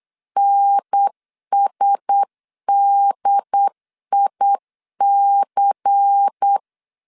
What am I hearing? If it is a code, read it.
Morse code, heard as a keyed tone, about 8 wpm, 800 Hz. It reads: NSDIC
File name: CW / DET / 1 6 1500C S C 8wpm 5219